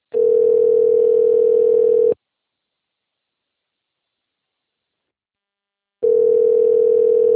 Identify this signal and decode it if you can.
Ringback tone (North America), 440 + 480 Hz.